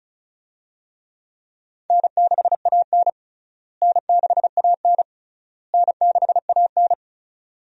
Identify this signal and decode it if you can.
Morse code, heard as a keyed tone, about 36 wpm, 700 Hz. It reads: N6AN N6AN N6AN
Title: USA Call Signs - 4 Characters - Repeated 3x - 35wpm